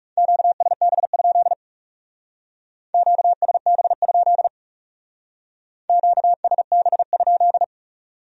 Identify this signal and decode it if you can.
Morse code, heard as a keyed tone, about 38 wpm, 700 Hz. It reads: QSB? QSB? QSB?